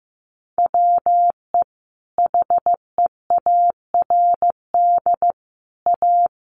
Morse code, heard as a keyed tone, about 15 wpm, 700 Hz. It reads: WE HEARD A
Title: Sentences from Top 1500 Words 15wpm